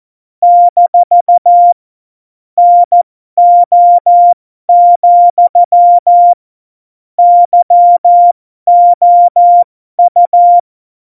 Morse code, heard as a keyed tone, about 14 wpm, spaced as for 12 wpm, 700 Hz. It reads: - NO, YOU
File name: Morsekurs / Witze